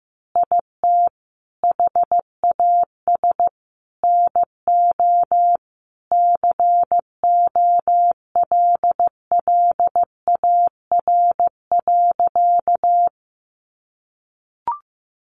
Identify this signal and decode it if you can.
Morse code, heard as a keyed tone, about 15 wpm, 700 Hz. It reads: IT HAS NO COLLAR.
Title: Easy Phrases 15wpm